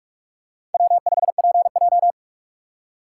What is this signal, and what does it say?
Morse code, heard as a keyed tone, about 40 wpm, 700 Hz. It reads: W5PJ